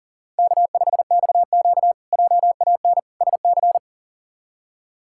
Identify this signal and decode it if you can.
Morse code, heard as a keyed tone, about 40 wpm, 700 Hz. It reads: K5XQ JAN SC
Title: K1USN SST QSO Practice - 40wpm